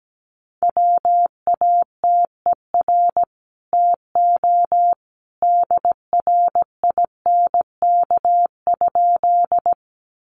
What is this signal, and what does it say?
Morse code, heard as a keyed tone, about 17 wpm, 700 Hz. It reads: WATER TO DRINK?
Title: Sentences from Top 1500 Words 17wpm